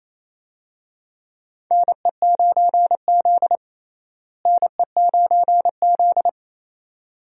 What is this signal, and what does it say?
Morse code, heard as a keyed tone, about 27 wpm, 700 Hz. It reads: NE9Z NE9Z
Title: USA Call Signs - 4 Characters - Repeated 2x - 28wpm